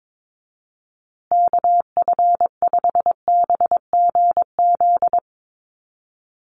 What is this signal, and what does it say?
Morse code, heard as a keyed tone, about 22 wpm, 700 Hz. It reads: KF5BGZ
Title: USA Call Signs Repeated 3x 22wpm